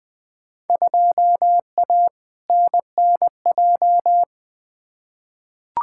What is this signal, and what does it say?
Morse code, heard as a keyed tone, about 20 wpm, 700 Hz. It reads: 2A NNJ
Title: Field Day Exchanges - 20wpm